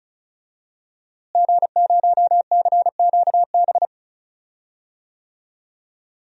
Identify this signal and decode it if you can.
Morse code, heard as a keyed tone, about 36 wpm, 700 Hz. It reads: G0CQB
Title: DXCC-100 Call Signs - 5 Characters - Repeated 2x - 35wpm